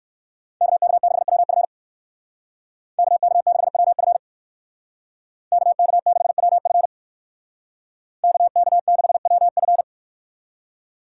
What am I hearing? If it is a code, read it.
Morse code, heard as a keyed tone, about 58 wpm, 700 Hz. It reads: KK6WF KK6WF KK6WF KK6WF